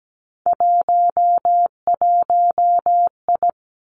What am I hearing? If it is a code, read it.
Morse code, heard as a keyed tone, about 17 wpm, 700 Hz. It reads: 11I
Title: Three Challenging Letter-Number Combinations 17wpm